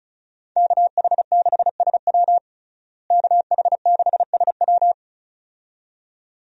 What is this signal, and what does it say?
Morse code, heard as a keyed tone, about 35 wpm, 700 Hz. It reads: KH6SW KH6SW
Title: DXCC-100 Call Signs - 5 Characters - Repeated 2x - 35wpm